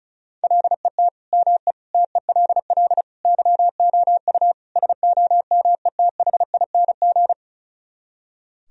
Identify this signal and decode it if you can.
Morse code, heard as a keyed tone, about 46 wpm, spaced as for 41 wpm, 700 Hz. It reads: LET ME TELL YOU SOMETHING
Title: Speed-Racing - Sets of 5 Words 35wpm